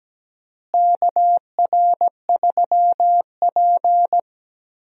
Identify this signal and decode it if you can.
Morse code, heard as a keyed tone, about 17 wpm, 700 Hz. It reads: KR3P
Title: USA Call Signs - 4 Characters - Repeated 2x - 17wpm